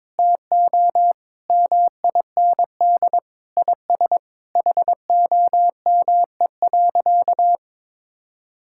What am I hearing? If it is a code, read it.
Morse code, heard as a keyed tone, about 22 wpm, 700 Hz. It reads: TO MIND IS HOME.
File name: Sentences with 7-Word Phrases - 22wpm